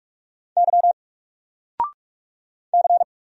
Morse code, heard as a keyed tone, about 44 wpm, 700 Hz. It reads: Y C
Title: Single Letter-Number 45wpm